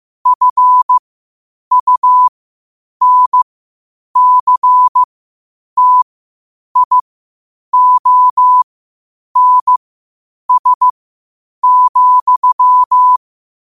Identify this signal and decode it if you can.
Morse code, heard as a keyed tone, about 15 wpm, spaced as for 8 wpm, 1000 Hz. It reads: FUNCTIONS,
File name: qotdfiles / 8wpm